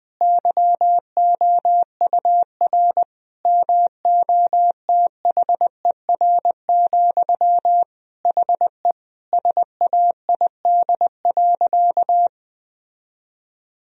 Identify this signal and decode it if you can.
Morse code, heard as a keyed tone, about 20 wpm, 700 Hz. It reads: YOUR MOTHER, HE SAID.
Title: Sentences from Top 1000 Words - Review Part 4 - 20wpm